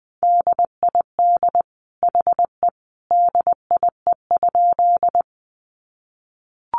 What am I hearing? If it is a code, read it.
Morse code, heard as a keyed tone, about 20 wpm, 700 Hz. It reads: DID HE DIE?
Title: Sentences from Top 1300 Words 20wpm